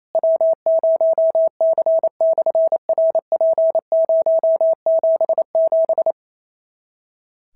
Morse code, heard as a keyed tone, about 28 wpm, 650 Hz. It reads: W0C/RP077
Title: SOTA QSO Practice 28wpm